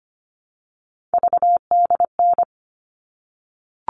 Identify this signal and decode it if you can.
Morse code, heard as a keyed tone, about 25 wpm, 700 Hz. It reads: VDN
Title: All Cut Numbers - 3-Digit 25wpm